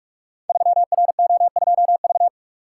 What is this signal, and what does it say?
Morse code, heard as a keyed tone, about 44 wpm, 700 Hz. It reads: 3RO2V
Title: Code Groups - 5 Letter:Numbers - 45wpm